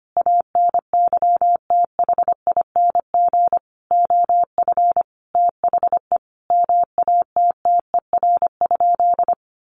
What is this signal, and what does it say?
Morse code, heard as a keyed tone, about 25 wpm, 700 Hz. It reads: ANYTHING OF THE MATTER?
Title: Sentences from Top 1000 Words 25wpm